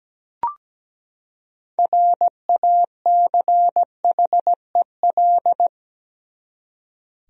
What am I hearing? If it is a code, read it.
Morse code, heard as a keyed tone, about 17 wpm, 700 Hz. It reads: RACHEL
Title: USA Common Names - 17wpm